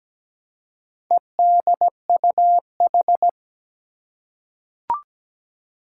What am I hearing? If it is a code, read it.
Morse code, heard as a keyed tone, about 17 wpm, 700 Hz. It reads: EDUH
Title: Four Challenging Letter-Number Combinations 17wpm